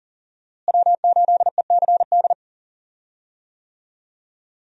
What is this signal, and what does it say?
Morse code, heard as a keyed tone, about 41 wpm, 700 Hz. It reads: W8ECD
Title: USA Call Signs - 5 Characters - Repeated 3x - 40wpm